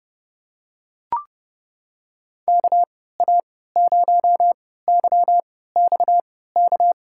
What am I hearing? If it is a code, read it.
Morse code, heard as a keyed tone, about 31 wpm, spaced as for 17 wpm, 700 Hz. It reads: KA0YXK
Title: USA Call Signs - 6 Characters - 3x Character Spacing - 30wpm